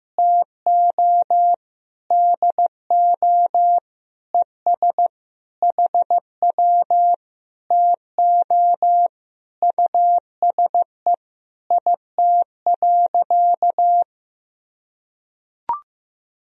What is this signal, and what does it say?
Morse code, heard as a keyed tone, about 15 wpm, 700 Hz. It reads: TO DO ES HW TO USE IT.
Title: Sentences from Top 500 Words and Abbreviations 15wpm